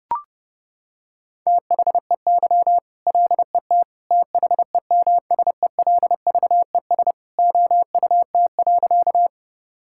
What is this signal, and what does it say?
Morse code, heard as a keyed tone, about 32 wpm, 700 Hz. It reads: THEY LET THEMSELVES OUT.
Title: Sentences from Top 1000 Words - Review Part 5 - 30wpm